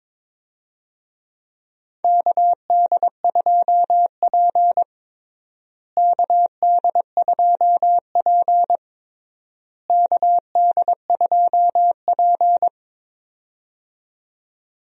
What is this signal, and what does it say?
Morse code, heard as a keyed tone, about 22 wpm, 700 Hz. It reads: KD2P KD2P KD2P
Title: USA Call Signs - 4 Characters - Repeated 3x - 22wpm